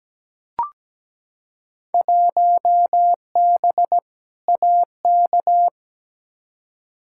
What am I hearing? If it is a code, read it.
Morse code, heard as a keyed tone, about 17 wpm, 700 Hz. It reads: 1B AK
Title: Field Day Exchanges Encore - 17wpm